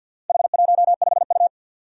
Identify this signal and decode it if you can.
Morse code, heard as a keyed tone, about 50 wpm, 700 Hz. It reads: H15U
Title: Four Challenging Letter-Number Combinations 50wpm